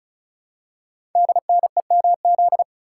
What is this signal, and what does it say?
Morse code, heard as a keyed tone, about 36 wpm, 700 Hz. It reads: DNEMZ
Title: Code Groups - 5 Letters - 35wpm